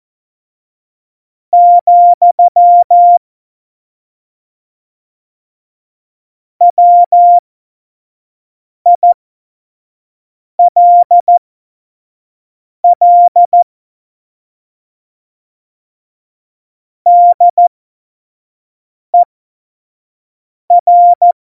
Morse code, heard as a keyed tone, about 14 wpm, spaced as for 5 wpm, 700 Hz. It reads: , WILL DER
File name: Morsekurs / Witze / Witz 30. Juli 2019, 25 F70